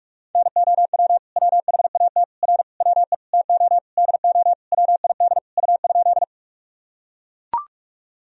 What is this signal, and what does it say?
Morse code, heard as a keyed tone, about 44 wpm, 700 Hz. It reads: NOW WHAT R WE TO DO WID U?
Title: Sentences from Top 100 Words and Abbreviations 45wpm